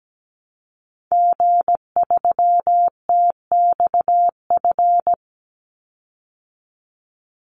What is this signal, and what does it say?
Morse code, heard as a keyed tone, about 17 wpm, 700 Hz. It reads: G3TXF
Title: Call Signs Repeated 3x - 17wpm